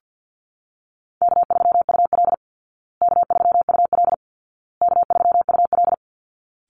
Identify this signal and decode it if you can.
Morse code, heard as a keyed tone, about 50 wpm, 700 Hz. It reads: K3UL K3UL K3UL
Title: Easy Call Signs Repeated 3x - 50wpm